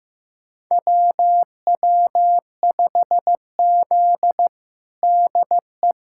Morse code, heard as a keyed tone, about 15 wpm, 700 Hz. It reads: WW5Z DE
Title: USA Callsign DE Callsign - 4 Characters - 15wpm